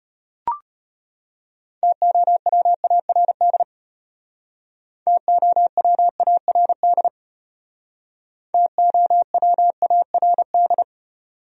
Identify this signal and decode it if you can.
Morse code, heard as a keyed tone, about 32 wpm, 700 Hz. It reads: TOWARD TOWARD TOWARD